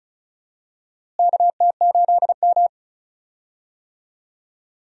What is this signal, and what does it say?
Morse code, heard as a keyed tone, about 34 wpm, 700 Hz. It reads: KT8M
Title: USA Call Signs - 4 Characters - Repeated 3x - 35wpm